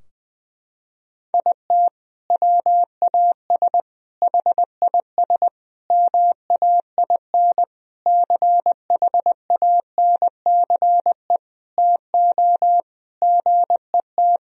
Morse code, heard as a keyed tone, about 20 wpm, 700 Hz. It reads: IT WAS HIS MAIN CHANCE TO GET
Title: Sentences from Top 1000 Words - Review Part 2 - 20wpm